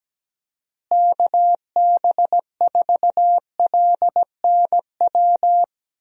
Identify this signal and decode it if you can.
Morse code, heard as a keyed tone, about 17 wpm, 700 Hz. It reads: KB4LNW
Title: USA Call Signs - 6 Characters - Repeated 2x - 17wpm